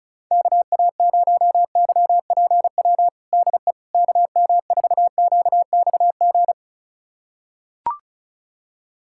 Morse code, heard as a keyed tone, about 36 wpm, 700 Hz. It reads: KA0YPW DE KM4QXG
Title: USA Callsign DE Callsign - 6 Characters - 35wpm